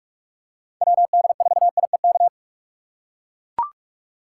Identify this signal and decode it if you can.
Morse code, heard as a keyed tone, about 44 wpm, 700 Hz. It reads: WD4IEK
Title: USA Call Signs - 6 Characters - 6x Character Spacing - 45wpm